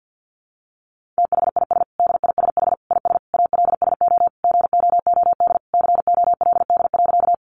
Morse code, heard as a keyed tone, about 50 wpm, 700 Hz. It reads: THIS DISH IS ALSO GOOD COLD.
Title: Sentences from Top 1300 Words 50wpm